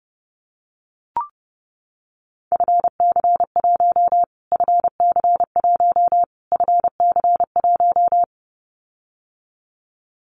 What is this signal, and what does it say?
Morse code, heard as a keyed tone, about 30 wpm, 700 Hz. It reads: FC1 FC1 FC1
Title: Callsign Prefixes for Top 25 DXCC - Repeated 3x - 30wpm